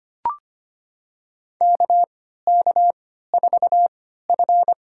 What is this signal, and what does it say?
Morse code, heard as a keyed tone, about 25 wpm, spaced as for 14 wpm, 700 Hz. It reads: KK4F
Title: USA Call Signs - 3x Character Spacing - 25wpm